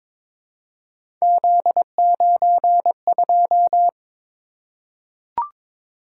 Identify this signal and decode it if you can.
Morse code, heard as a keyed tone, about 22 wpm, 700 Hz. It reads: Z92
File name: practice / 22wpm / Three Letter-Number Combinations - 22wpm